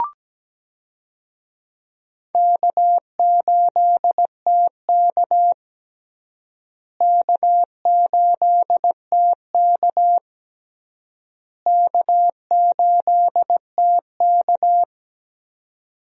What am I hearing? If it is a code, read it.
Morse code, heard as a keyed tone, about 17 wpm, 700 Hz. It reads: K8TK K8TK K8TK
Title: USA Call Signs - 4 Characters - Repeated 3x - 17wpm